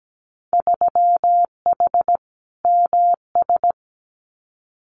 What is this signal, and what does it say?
Morse code, heard as a keyed tone, about 17 wpm, 700 Hz. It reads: 3H MS
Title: Winter Field Day Exchanges Encore 17wpm